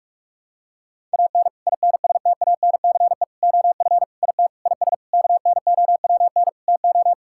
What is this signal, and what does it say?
Morse code, heard as a keyed tone, about 41 wpm, 700 Hz. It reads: AN INSTANCE OF IT IS KNOWN TO
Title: Sentences from Top 1000 Words - Review Part 5 - 45wpm